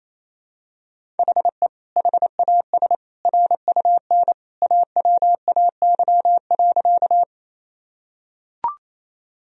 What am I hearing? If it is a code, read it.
Morse code, heard as a keyed tone, about 28 wpm, 700 Hz. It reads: HE HAS RUN AWAY.
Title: Sentences from Top 1000 Words - Review Part 1 - 28wpm